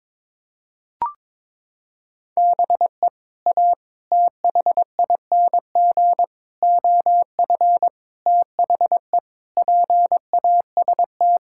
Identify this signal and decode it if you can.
Morse code, heard as a keyed tone, about 22 wpm, 700 Hz. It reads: BE A THING OF THE PAST
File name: practice / 22wpm / Sets of 6 Words - 22wpm